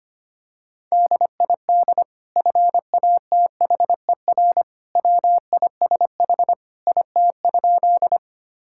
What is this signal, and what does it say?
Morse code, heard as a keyed tone, about 25 wpm, 700 Hz. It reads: DID FATHER WISH IT?
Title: Sentences from Top 1600 Words 25wpm